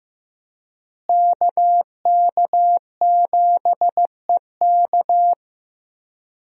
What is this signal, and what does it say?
Morse code, heard as a keyed tone, about 16 wpm, 700 Hz. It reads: KK7EK
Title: USA Call Signs - 5 Characters - Repeated 2x - 15wpm